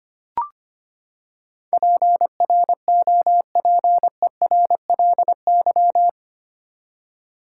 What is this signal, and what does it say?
Morse code, heard as a keyed tone, about 25 wpm, 700 Hz. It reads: PROPERLY
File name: Eight Letter Words - 25wpm